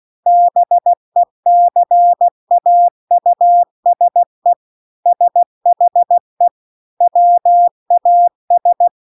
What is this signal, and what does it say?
Morse code, heard as a keyed tone, about 16 wpm, 700 Hz. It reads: BECAUSE SHE WAS